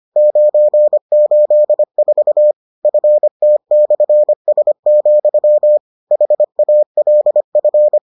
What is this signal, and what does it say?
Morse code, heard as a keyed tone, about 25 wpm, 600 Hz. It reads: 984 FT/S, HALF